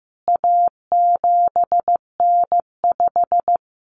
Morse code, heard as a keyed tone, about 15 wpm, 700 Hz. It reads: A7N5
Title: CWT Exchange 15wpm